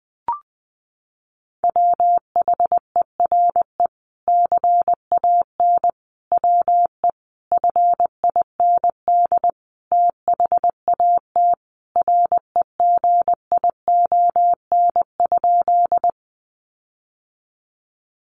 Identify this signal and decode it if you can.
Morse code, heard as a keyed tone, about 20 wpm, 700 Hz. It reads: WHERE CAN WE FIND THAT REGION?
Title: Sentences from Top 1500 Words 20wpm